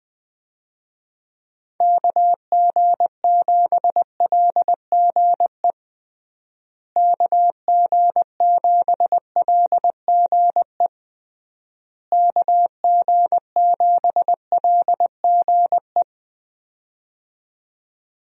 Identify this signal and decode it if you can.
Morse code, heard as a keyed tone, about 20 wpm, 700 Hz. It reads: KG7LGE KG7LGE KG7LGE
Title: USA Call Signs - 6 Characters - Repeated 3x - 20wpm